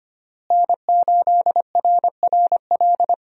Morse code, heard as a keyed tone, about 25 wpm, 700 Hz. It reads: N8RRL
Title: USA Call Signs - 5 Characters - Repeated 2x - 25wpm